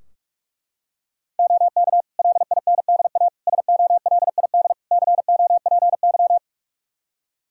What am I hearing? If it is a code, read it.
Morse code, heard as a keyed tone, about 41 wpm, 700 Hz. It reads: OK LINDA SOLID COPY
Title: Practice QSO - Exchange 3 - More Info - 45wpm